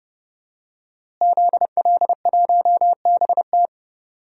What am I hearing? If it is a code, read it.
Morse code, heard as a keyed tone, about 30 wpm, 700 Hz. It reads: ZL1BT
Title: DXCC-100 Call Signs - 5 Characters - Repeated 2x - 30wpm